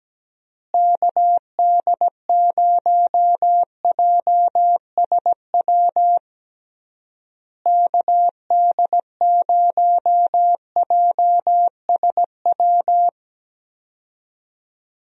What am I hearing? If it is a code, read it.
Morse code, heard as a keyed tone, about 17 wpm, 700 Hz. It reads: KD0JSW KD0JSW
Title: USA Call Signs - 6 Characters - Repeated 3x - 17wpm